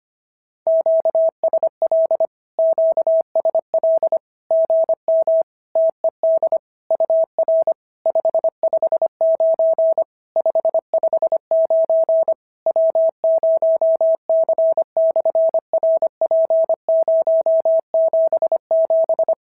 Morse code, heard as a keyed tone, about 25 wpm, 650 Hz. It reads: QSL QSL GM TED UR 559 559 W0C/RP077
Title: SOTA QSO Practice 25wpm